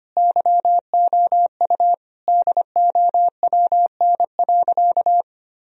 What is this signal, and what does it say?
Morse code, heard as a keyed tone, about 25 wpm, 700 Hz. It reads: YOU DOWN.
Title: Sentences from Top 1000 Words - Review Part 2 - 25wpm